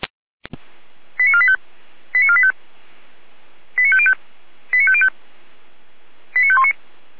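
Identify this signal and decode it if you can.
Five-tone selective call (ZVEI), heard as quick runs of five tones, tones of about 70 ms. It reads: 89485 89485 89505 89505 89519
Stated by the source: Five-tone ZVEI selective call received on an SDR.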